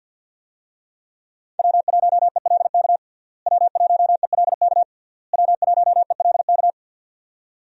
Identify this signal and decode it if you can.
Morse code, heard as a keyed tone, about 51 wpm, 700 Hz. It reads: W1ELK W1ELK W1ELK
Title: USA Call Signs - 5 Characters - Repeated 3x - 50wpm